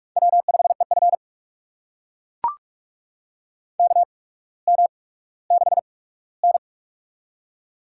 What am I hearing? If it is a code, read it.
Morse code, heard as a keyed tone, about 44 wpm, spaced as for 19 wpm, 700 Hz. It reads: W5EF KM6N
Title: USA Call Signs - 4 Characters - 8x Character Spacing - 45wpm